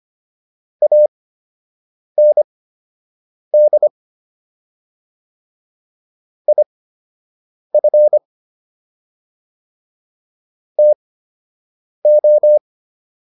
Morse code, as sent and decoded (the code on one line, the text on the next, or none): .- -. -.. / .. ..-. / - ---
AND IF TO